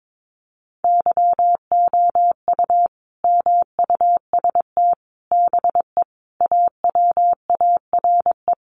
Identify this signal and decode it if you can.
Morse code, heard as a keyed tone, about 22 wpm, 700 Hz. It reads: YOU MUST BE AWARE
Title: Sentences from Top 1200 Words 22wpm